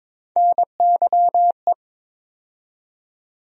Morse code, heard as a keyed tone, about 22 wpm, 700 Hz. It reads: NYE
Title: USA Call Sign Postfixes - 3 Letters 22wpm